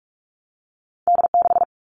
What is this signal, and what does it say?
Morse code, heard as a keyed tone, about 46 wpm, 700 Hz. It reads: D6
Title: All Cut Numbers - 2-Digit 45wpm